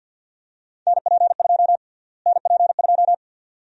Morse code, heard as a keyed tone, about 52 wpm, 700 Hz. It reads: NP2 NP2
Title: USA Call Sign Prefixes - Repeated 2x - 50wpm